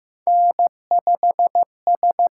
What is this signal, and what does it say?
Morse code, heard as a keyed tone, about 15 wpm, 700 Hz. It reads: N5S
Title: USA Call Signs - 5 Characters - 8x Character Spacing - 15wpm